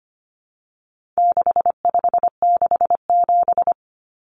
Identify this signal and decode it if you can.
Morse code, heard as a keyed tone, about 25 wpm, 700 Hz. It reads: 6567
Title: Numbers - 4-Digit 25wpm